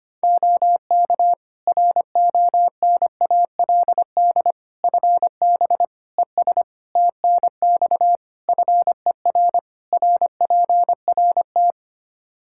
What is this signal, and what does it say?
Morse code, heard as a keyed tone, about 25 wpm, 700 Hz. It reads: OK RONALD FB ES TNX FER RPRT
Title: Practice QSO - Exchange 2 - Info - 25wpm